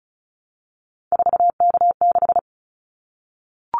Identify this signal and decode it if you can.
Morse code, heard as a keyed tone, about 34 wpm, 700 Hz. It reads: 4K6
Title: Callsign Prefixes for Top 100 DXCC - 35wpm